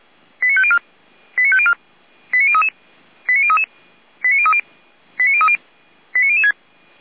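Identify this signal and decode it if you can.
Five-tone selective call (ZVEI), heard as quick runs of five tones, tones of about 70 ms. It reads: 89504 89504 89033 89033 89030 89030 89006
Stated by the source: ZVEI five-tone call received by SDR, from the Feuerwehr Orlamünde audio server.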